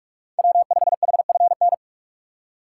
Morse code, heard as a keyed tone, about 47 wpm, 700 Hz. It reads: W5HFN
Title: USA Call Signs - 5 Characters - 6x Character Spacing - 45wpm